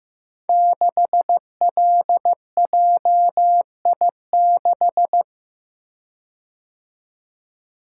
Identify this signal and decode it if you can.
Morse code, heard as a keyed tone, about 15 wpm, 700 Hz. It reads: 6LJI6
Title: Code Groups - 5 Letter:Numbers - 15wpm